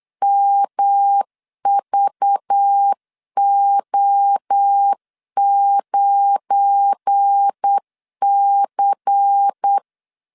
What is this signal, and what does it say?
Morse code, heard as a keyed tone, about 8 wpm, 800 Hz. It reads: MVO9C
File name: LEZIONI / 7 GF9Z / 7 GF9 8wpm 3545